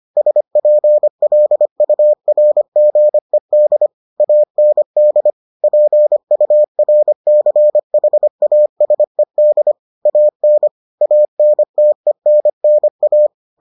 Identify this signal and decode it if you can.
Morse code, heard as a keyed tone, about 25 wpm, 600 Hz. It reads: SPLURGED AND PURCHASED AN ANTENNA